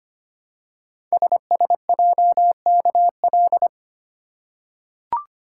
Morse code, heard as a keyed tone, about 25 wpm, 700 Hz. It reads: SSJKL
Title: Code Groups - 5 Letters - 25wpm